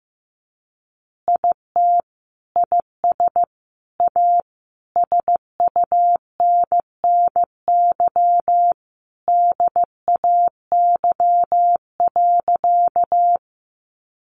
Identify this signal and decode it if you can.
Morse code, heard as a keyed tone, about 15 wpm, 700 Hz. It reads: IT IS A SUNNY DAY.
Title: Additional Easy Phrases 15wpm